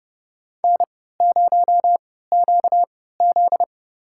Morse code, heard as a keyed tone, about 30 wpm, spaced as for 17 wpm, 700 Hz. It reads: N0QZ
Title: USA Call Signs - 4 Characters - 3x Character Spacing - 30wpm